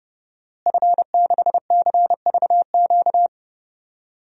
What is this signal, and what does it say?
Morse code, heard as a keyed tone, about 30 wpm, 700 Hz. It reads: F6CVQ
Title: DXCC-100 Call Signs - 5 Characters - Repeated 3x - 30wpm